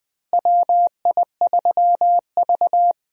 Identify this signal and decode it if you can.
Morse code, heard as a keyed tone, about 20 wpm, 700 Hz. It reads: WI3V
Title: USA Call Signs - 4 Characters - Repeated 2x - 20wpm